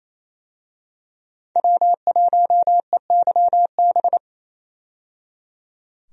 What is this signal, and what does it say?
Morse code, heard as a keyed tone, about 28 wpm, 700 Hz. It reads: W1EYB
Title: USA Call Signs - 5 Characters - 28wpm